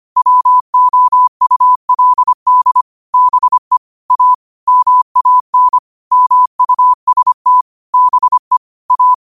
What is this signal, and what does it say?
Morse code, heard as a keyed tone, about 25 wpm, 1000 Hz. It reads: WOULD BE A MAN MUST BE A
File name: 25wpm